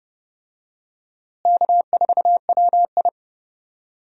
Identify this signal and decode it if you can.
Morse code, heard as a keyed tone, about 31 wpm, 700 Hz. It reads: K4WI
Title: USA Call Signs - 4 Characters - Repeated 2x - 30wpm